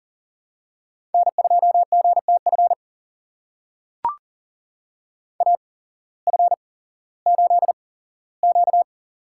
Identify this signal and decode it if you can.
Morse code, heard as a keyed tone, about 40 wpm, spaced as for 19 wpm, 700 Hz. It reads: N2GTF AF8Q
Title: USA Call Signs - 8x Character Spacing - 40wpm